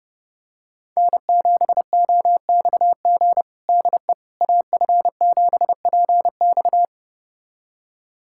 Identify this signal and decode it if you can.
Morse code, heard as a keyed tone, about 32 wpm, 700 Hz. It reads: N7OXG DE AF7PX
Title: USA Callsign DE Callsign - 5 Characters - 30wpm